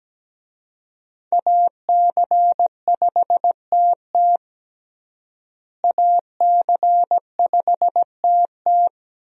Morse code, heard as a keyed tone, about 17 wpm, 700 Hz. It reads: AC5TT AC5TT
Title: USA Call Signs - 5 Characters - Repeated 3x - 17wpm